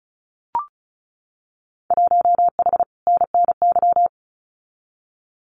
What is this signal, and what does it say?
Morse code, heard as a keyed tone, about 34 wpm, 700 Hz. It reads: 1H NNY
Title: Winter Field Day Exchanges 35wpm